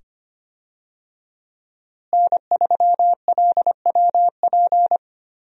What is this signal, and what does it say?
Morse code, heard as a keyed tone, about 25 wpm, 700 Hz. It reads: N3LWP
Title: USA Call Signs - 5 Characters - Repeated 2x - 25wpm